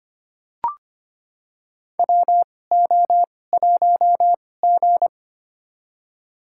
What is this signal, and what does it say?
Morse code, heard as a keyed tone, about 25 wpm, spaced as for 18 wpm, 700 Hz. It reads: WO1G
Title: USA Call Signs - 4 Characters - 2x Character Spacing - 25wpm